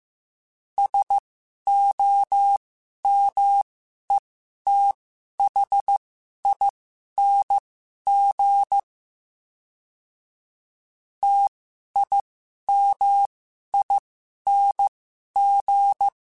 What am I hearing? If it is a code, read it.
Morse code, heard as a keyed tone, about 15 wpm, spaced as for 11 wpm, 800 Hz. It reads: SOMETHING TIMING
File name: Special INGs-mp3s / Special INGs-mp3s / Sngl Ing @10